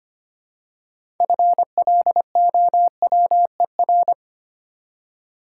Morse code, heard as a keyed tone, about 25 wpm, 700 Hz. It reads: FLOWER
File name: practice / 25wpm / Easy Words with Hints - 25wpm